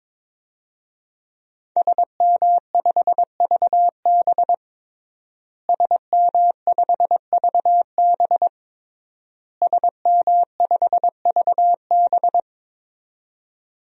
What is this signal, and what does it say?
Morse code, heard as a keyed tone, about 21 wpm, 700 Hz. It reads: SM5VB SM5VB SM5VB
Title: DXCC-100 Call Signs - 5 Characters - Repeated 3x - 22wpm